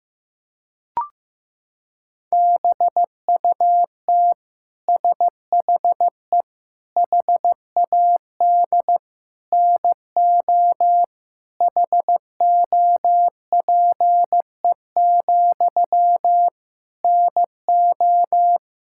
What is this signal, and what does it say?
Morse code, heard as a keyed tone, about 16 wpm, 700 Hz. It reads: BUT SHE HAD NO HOPE, NO
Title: Sentences from Top 1600 Words 15wpm